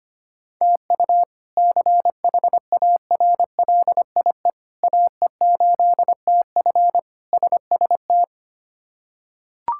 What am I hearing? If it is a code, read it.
Morse code, heard as a keyed tone, about 25 wpm, 700 Hz. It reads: TU CHARLIE AE8TF SST
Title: K1USN SST QSO Practice - 25wpm